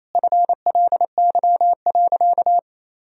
Morse code, heard as a keyed tone, about 28 wpm, 700 Hz. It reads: FLY.
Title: Sentences from Top 1000 Words - Review Part 2 - 28wpm